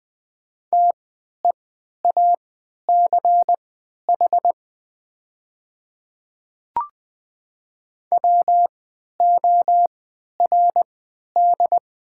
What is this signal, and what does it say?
Morse code, heard as a keyed tone, about 20 wpm, spaced as for 11 wpm, 700 Hz. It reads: TEACH WORD
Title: 3x Character Spacing - Top 500 Words - 20wpm